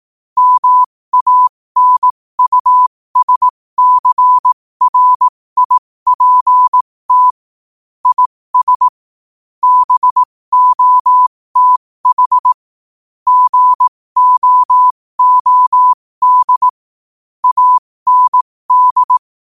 Morse code, heard as a keyed tone, about 18 wpm, spaced as for 15 wpm, 1000 Hz. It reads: MANUSCRIPT IS BOTH GOOD AND
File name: qotdfiles / 15wpm